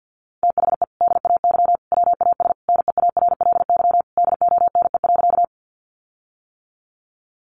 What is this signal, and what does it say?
Morse code, heard as a keyed tone, about 50 wpm, 700 Hz. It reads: THE DAY WAS NEARLY DONE.
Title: Sentences from Top 1300 Words 50wpm